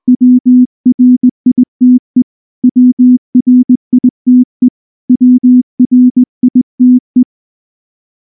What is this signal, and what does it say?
Morse code, heard as a keyed tone, about 19 wpm, 250 Hz. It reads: WRITE WRITE WRITE
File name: morse / morse-training-20WPM-5